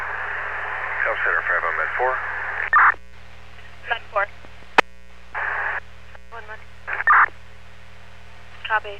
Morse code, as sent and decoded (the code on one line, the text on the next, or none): none